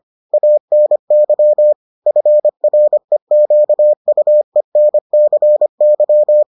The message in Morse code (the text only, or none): ANY FREQUENCY